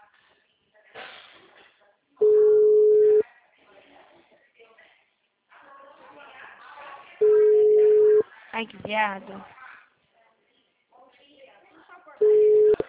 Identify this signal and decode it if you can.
Ringback tone (Europe, 425 Hz), with speech.